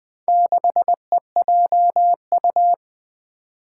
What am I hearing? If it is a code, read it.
Morse code, heard as a keyed tone, about 20 wpm, 700 Hz. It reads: 6EJU
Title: Four Challenging Letter-Number Combinations 20wpm